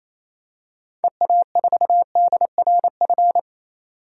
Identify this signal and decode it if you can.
Morse code, heard as a keyed tone, about 28 wpm, 700 Hz. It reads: EA4DRF
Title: DXCC-100 Call Signs - 6 Characters - 28wpm